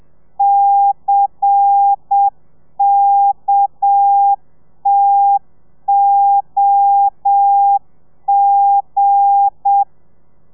Morse code, heard as a keyed tone, about 7 wpm, 800 Hz. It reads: CKTOG
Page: Lezione 1: audio a 30 CPM